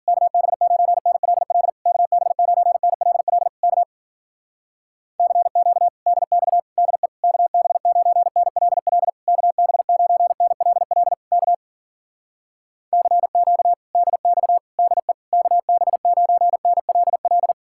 Morse code, heard as a keyed tone, about 55 wpm, 700 Hz. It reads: KB9NLL KB9NLL K CQ DX DE KB9NLL KB9NLL K CQ DX DE KB9NLL